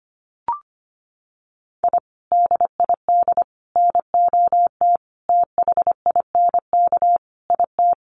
Morse code, heard as a keyed tone, about 25 wpm, 700 Hz. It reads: I DID NOT THINK IT
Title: Sets of 5 Words Encore2 25wpm